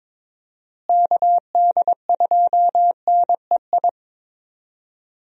Morse code, heard as a keyed tone, about 22 wpm, 700 Hz. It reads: KD2NEI
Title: USA Call Signs - 6 Characters - Repeated 2x - 22wpm